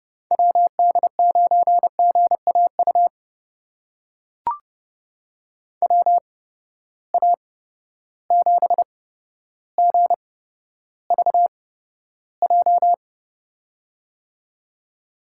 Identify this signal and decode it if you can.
Morse code, heard as a keyed tone, about 30 wpm, spaced as for 13 wpm, 700 Hz. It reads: WD9GAU WA7GVJ
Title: USA Call Signs - 6 Characters - 8x Character Spacing - 30wpm